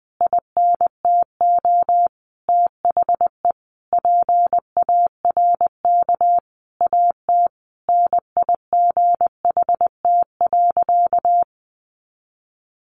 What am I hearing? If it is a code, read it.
Morse code, heard as a keyed tone, about 20 wpm, 700 Hz. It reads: INTO THE PARK AT NIGHT.
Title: Sentences from Top 1300 Words 20wpm